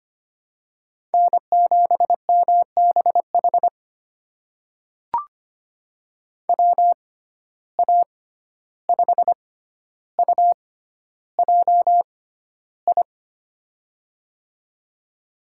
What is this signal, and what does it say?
Morse code, heard as a keyed tone, about 25 wpm, spaced as for 12 wpm, 700 Hz. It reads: N7MBH WA5UJI
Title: USA Call Signs - 6x Character Spacing - 25wpm